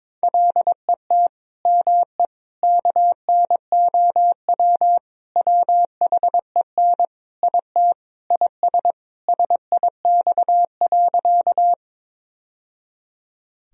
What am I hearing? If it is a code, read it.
Morse code, heard as a keyed tone, about 22 wpm, 700 Hz. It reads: LET ME KNOW WHEN IT IS SIX.
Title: Sentences from Top 1000 Words - Review Part 2 - 22wpm